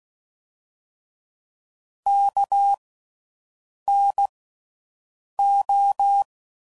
Morse code, heard as a keyed tone, about 16 wpm, spaced as for 6 wpm, 800 Hz. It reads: KNO
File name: Slow With Faster Ings